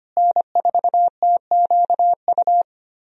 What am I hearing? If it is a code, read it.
Morse code, heard as a keyed tone, about 25 wpm, 700 Hz. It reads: N4TQU
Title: USA Call Signs - 5 Characters - Repeated 2x - 25wpm